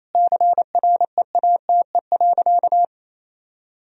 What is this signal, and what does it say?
Morse code, heard as a keyed tone, about 28 wpm, 700 Hz. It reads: CREATE.
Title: Sentences from Top 1000 Words - Review Part 3 - 28wpm